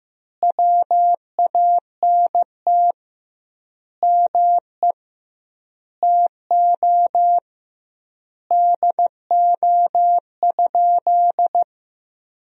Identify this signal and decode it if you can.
Morse code, heard as a keyed tone, about 15 wpm, 700 Hz. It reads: WANT ME TO DO?
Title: 2x Word-Spacing - Sentences from Top 200 Words - 15wpm